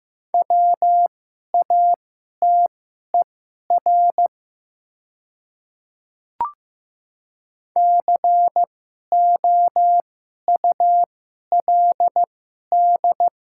Morse code, heard as a keyed tone, about 15 wpm, spaced as for 11 wpm, 700 Hz. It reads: WATER COULD
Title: 2x Character Spacing - Top 100 Words - 15wpm